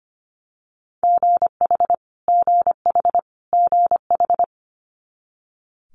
Morse code, heard as a keyed tone, about 25 wpm, 700 Hz. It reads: GH GH GH
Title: Multi-letter Phonemes Repeated 3x 25wpm